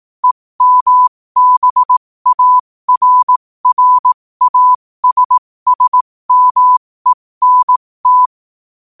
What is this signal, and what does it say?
Morse code, heard as a keyed tone, about 18 wpm, spaced as for 15 wpm, 1000 Hz. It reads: EMBARRASSMENT
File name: qotdfiles / 15wpm